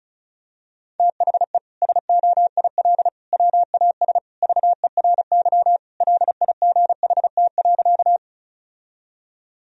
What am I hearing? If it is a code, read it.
Morse code, heard as a keyed tone, about 36 wpm, 700 Hz. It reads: THE SOIL WAS VERY LIGHT.
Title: Sentences from Top 1000 Words - Review Part 4 - 35wpm